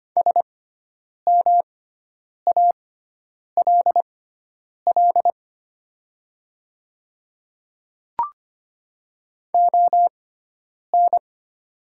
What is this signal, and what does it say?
Morse code, heard as a keyed tone, about 25 wpm, spaced as for 9 wpm, 700 Hz. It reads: SMALL ON
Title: 6x Character Spacing - Top 200 Words - 25wpm